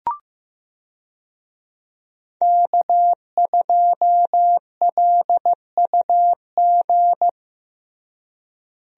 Morse code, heard as a keyed tone, about 15 wpm, 700 Hz. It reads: K2LUG
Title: USA Call Signs - 5 Characters - Repeated 3x - 15wpm